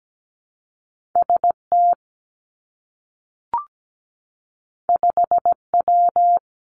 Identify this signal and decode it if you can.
Morse code, heard as a keyed tone, about 17 wpm, 700 Hz. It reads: ST 5W
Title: Two Challenging Letter-Number Combinations 17wpm